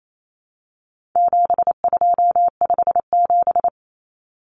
Morse code, heard as a keyed tone, about 28 wpm, 700 Hz. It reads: 7257
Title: Numbers - 4-Digit 28wpm